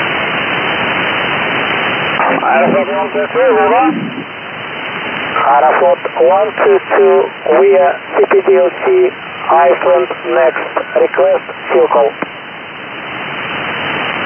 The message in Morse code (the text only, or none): none